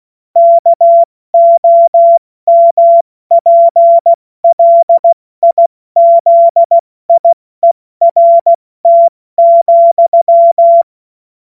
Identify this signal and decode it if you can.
Morse code, heard as a keyed tone, about 16 wpm, 680 Hz. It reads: KOMPLIZIERT,
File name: Morsekurs / Witze / Witz 14. Dezember 2021, 70 F80